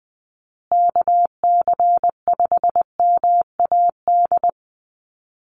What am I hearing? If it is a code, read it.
Morse code, heard as a keyed tone, about 20 wpm, 700 Hz. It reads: KC5MAD
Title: USA Call Signs Repeated 3x 20wpm